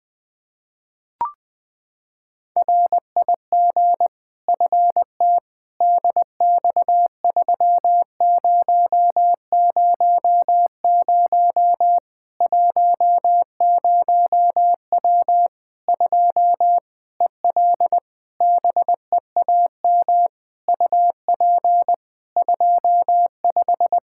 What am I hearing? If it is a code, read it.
Morse code, heard as a keyed tone, about 21 wpm, 700 Hz. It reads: RIG FT DX3000 10W 2 EL BEAM UP 25
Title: Practice QSO - Exchange 2 - Info - 20wpm